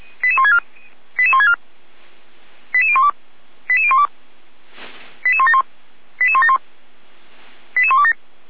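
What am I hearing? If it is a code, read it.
Five-tone selective call (ZVEI), heard as quick runs of five tones, tones of about 70 ms. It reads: 88164 88012 88012 88171 88171 88127
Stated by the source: ZVEI five-tone call 88199 received by SDR, archived on the Feuerwehr Orlamünde audio server.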